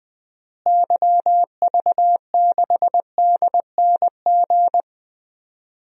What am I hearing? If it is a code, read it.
Morse code, heard as a keyed tone, about 20 wpm, 700 Hz. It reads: YV6DNG
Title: DXCC-100 Call Signs - 6 Characters - Repeated 2x - 20wpm